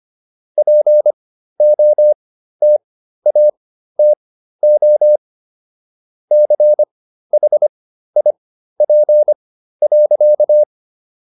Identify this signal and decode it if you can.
Morse code, heard as a keyed tone, about 25 wpm, spaced as for 13 wpm, 600 Hz. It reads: POTATO CHIP.